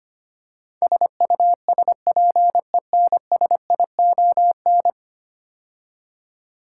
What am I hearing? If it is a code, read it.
Morse code, heard as a keyed tone, about 25 wpm, 700 Hz. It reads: SUSPENSION
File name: 10 Letter Words - First 3 Letters Given - 25wpm